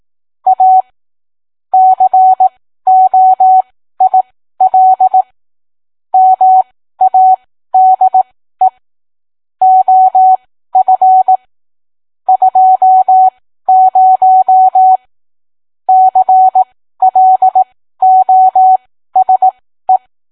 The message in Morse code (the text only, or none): A COIL MADE OF 20 CLOSE